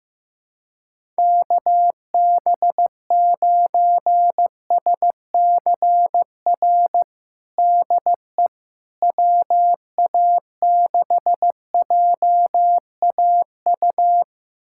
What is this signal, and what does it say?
Morse code, heard as a keyed tone, about 15 wpm, 700 Hz. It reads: KB9SCR DE WA6JAU
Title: USA Callsign DE Callsign - 6 Characters - 15wpm